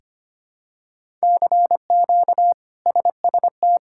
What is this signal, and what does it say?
Morse code, heard as a keyed tone, about 25 wpm, 700 Hz. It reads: CQ SST
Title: K1USN SST QSO Practice - 25wpm